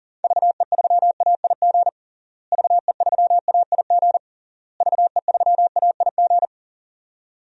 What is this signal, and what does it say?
Morse code, heard as a keyed tone, about 37 wpm, 700 Hz. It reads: VE3AIG VE3AIG VE3AIG
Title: DXCC-100 Call Signs - 6 Characters - Repeated 3x - 40wpm